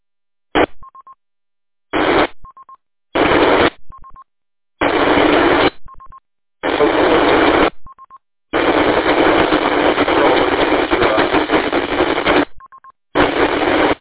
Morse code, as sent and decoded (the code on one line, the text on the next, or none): none